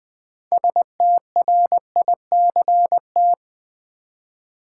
Morse code, heard as a keyed tone, about 21 wpm, 700 Hz. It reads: STRICT
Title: Six Letter Words - 20wpm